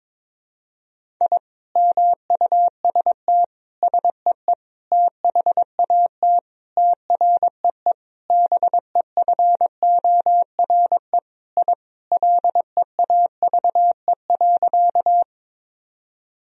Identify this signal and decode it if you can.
Morse code, heard as a keyed tone, about 22 wpm, 700 Hz. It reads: I MUST SEE THAT TREE BEFORE I LEAVE.
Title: Sentences from Top 1000 Words - Review Part 1 - 22wpm